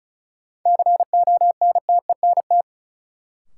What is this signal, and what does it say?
Morse code, heard as a keyed tone, about 34 wpm, 700 Hz. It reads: CONTENT
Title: Sentences from Top 900 Words - New Word Drill - 35wpm